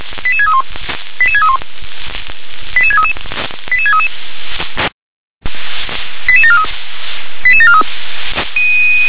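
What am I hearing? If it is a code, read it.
Five-tone selective call (ZVEI), heard as quick runs of five tones, tones of about 70 ms. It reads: 88631 88631 88633 88633 88743 88743
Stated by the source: ZVEI five-tone selective call 88731 received by SDR.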